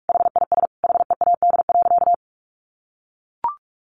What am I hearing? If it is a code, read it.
Morse code, heard as a keyed tone, about 44 wpm, 700 Hz. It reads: HIS HEAD.
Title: Sentences from Top 1100 Words 45wpm